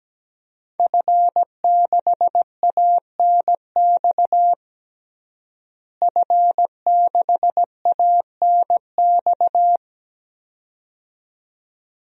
Morse code, heard as a keyed tone, about 17 wpm, 700 Hz. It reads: F6ANX F6ANX
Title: DXCC-100 Call Signs - 5 Characters - Repeated 2x - 17wpm